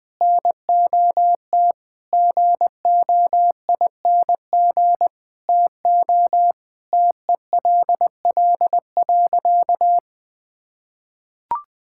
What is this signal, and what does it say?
Morse code, heard as a keyed tone, about 20 wpm, 700 Hz. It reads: NOT GOING TO TELL.
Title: Sentences with 5-Word Phrases - 20wpm